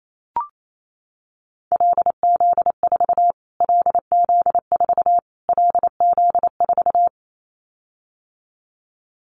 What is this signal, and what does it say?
Morse code, heard as a keyed tone, about 28 wpm, 700 Hz. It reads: LZ4 LZ4 LZ4
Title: Callsign Prefixes for Top 25 DXCC - Repeated 3x - 28wpm